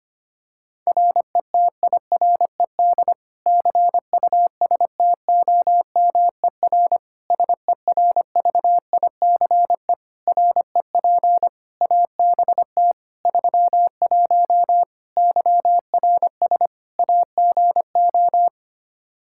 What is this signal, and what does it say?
Morse code, heard as a keyed tone, about 25 wpm, 700 Hz. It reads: RETIRED CUSTOMER SERVICE REP ABT 31 YRS AGO
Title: Practice QSO - Exchange 3 - More Info - 25wpm